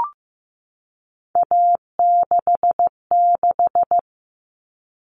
Morse code, heard as a keyed tone, about 15 wpm, 700 Hz. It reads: A66
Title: All Cut Numbers - 3-Digit 15wpm